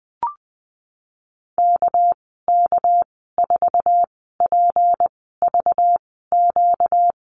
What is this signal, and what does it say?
Morse code, heard as a keyed tone, about 20 wpm, 700 Hz. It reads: K K 4 P V Q
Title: 238 Lesson 38g - Letter q - Call Signs 20wpm